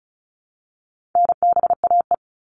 Morse code, heard as a keyed tone, about 34 wpm, 700 Hz. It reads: NBAE
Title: All Cut Numbers - 4-Digit 35wpm